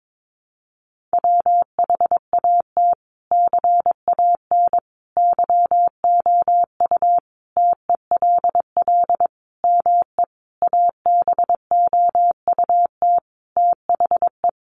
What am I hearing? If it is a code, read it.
Morse code, heard as a keyed tone, about 22 wpm, 700 Hz. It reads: WHAT CAN YOU TELL ME ABOUT THE
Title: Sentences from Top 1500 Words 22wpm